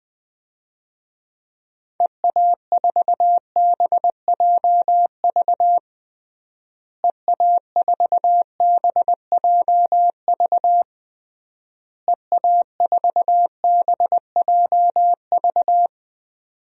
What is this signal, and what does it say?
Morse code, heard as a keyed tone, about 21 wpm, 700 Hz. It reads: EA4BJV EA4BJV EA4BJV
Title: DXCC-100 Call Signs - 6 Characters - Repeated 3x - 20wpm